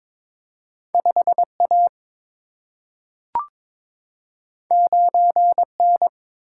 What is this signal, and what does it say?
Morse code, heard as a keyed tone, about 22 wpm, 700 Hz. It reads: 5A 9N
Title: Single-Letter Single-Number 22wpm